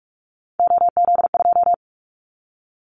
Morse code, heard as a keyed tone, about 44 wpm, 700 Hz. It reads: OZ2
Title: Callsign Prefixes for Top 50 DXCC - 45wpm